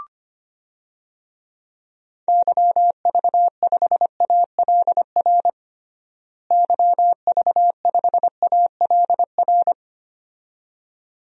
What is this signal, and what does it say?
Morse code, heard as a keyed tone, about 25 wpm, 700 Hz. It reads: YV5ALR YV5ALR
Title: DXCC-100 Call Signs - 6 Characters - Repeated 2x - 25wpm